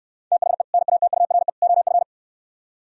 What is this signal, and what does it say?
Morse code, heard as a keyed tone, about 69 wpm, 700 Hz. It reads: THE NATURE OF